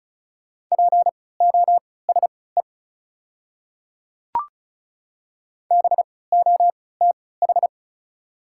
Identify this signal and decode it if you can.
Morse code, heard as a keyed tone, about 36 wpm, spaced as for 20 wpm, 690 Hz. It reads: POSE BOTH
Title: 3x Character Spacing - Top 500 Words - 35wpm